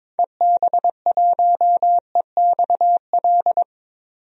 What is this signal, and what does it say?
Morse code, heard as a keyed tone, about 22 wpm, 700 Hz. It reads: EB1EXL
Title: DXCC-100 Call Signs - 6 Characters - 22wpm